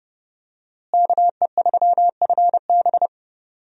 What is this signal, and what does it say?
Morse code, heard as a keyed tone, about 30 wpm, 700 Hz. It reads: KE3FB
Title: USA Call Signs - 5 Characters - Repeated 2x - 30wpm